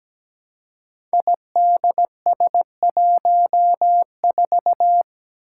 Morse code, heard as a keyed tone, about 17 wpm, 700 Hz. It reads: IDS14
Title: Five Challenging Letter-Number Combinations 17wpm